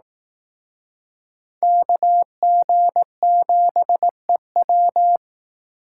Morse code, heard as a keyed tone, about 19 wpm, 700 Hz. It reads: KG7EW